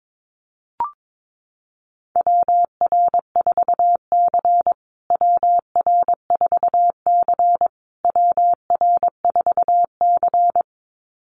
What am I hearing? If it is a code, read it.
Morse code, heard as a keyed tone, about 22 wpm, 700 Hz. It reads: WR4C WR4C WR4C
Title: USA Call Signs Repeated 3x 22wpm